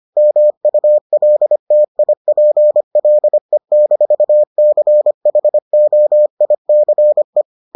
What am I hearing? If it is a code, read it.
Morse code, heard as a keyed tone, about 25 wpm, 600 Hz. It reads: MULTIPLE-CHOICE